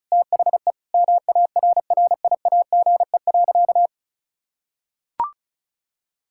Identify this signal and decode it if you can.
Morse code, heard as a keyed tone, about 36 wpm, 700 Hz. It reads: THE MARRIAGE.
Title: Sentences from Top 1000 Words - Review Part 6 - 35wpm